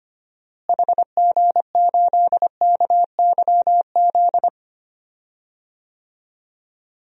Morse code, heard as a keyed tone, about 25 wpm, 700 Hz. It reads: HG8KYZ
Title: DXCC-100 Call Signs - 6 Characters - Repeated 2x - 25wpm